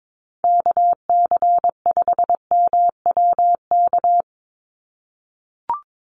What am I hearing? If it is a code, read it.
Morse code, heard as a keyed tone, about 22 wpm, 700 Hz. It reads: KC5MWK
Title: USA Call Signs 22wpm